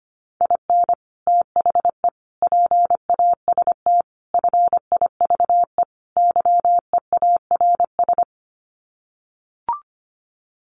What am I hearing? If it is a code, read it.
Morse code, heard as a keyed tone, about 25 wpm, 700 Hz. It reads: IN THE PAST FIVE YEARS
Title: Sets of 5 Words Encore 25wpm